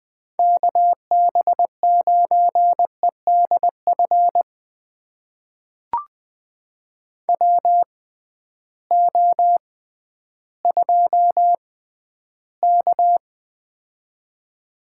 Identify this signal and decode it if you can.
Morse code, heard as a keyed tone, about 20 wpm, spaced as for 12 wpm, 700 Hz. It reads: KB9EDF WO2K
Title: USA Call Signs - 6x Character Spacing - 20wpm